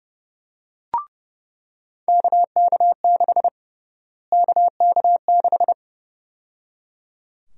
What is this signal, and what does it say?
Morse code, heard as a keyed tone, about 29 wpm, 700 Hz. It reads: KK6 KK6
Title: USA Call Sign Prefixes - Repeated 2x - 30wpm